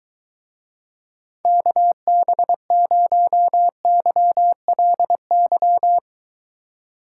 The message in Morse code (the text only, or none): KB0YLY